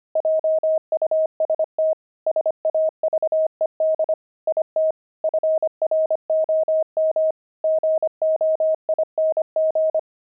Morse code, heard as a keyed tone, about 25 wpm, 630 Hz. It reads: JUST SAVED IT FROM GOING
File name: pangram-fs-25-cs-25-axle